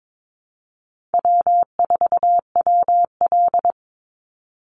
Morse code, heard as a keyed tone, about 22 wpm, 700 Hz. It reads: W4WL
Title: USA Call Signs 22wpm